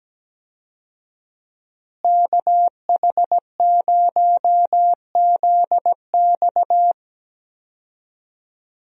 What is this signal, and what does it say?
Morse code, heard as a keyed tone, about 17 wpm, 700 Hz. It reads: KH0ZX
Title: USA Call Signs - 5 Characters - Repeated 3x - 17wpm